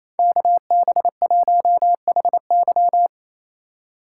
Morse code, heard as a keyed tone, about 35 wpm, 700 Hz. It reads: KB1HY
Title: Speed Racing - USA 2x2 Call Signs 28wpm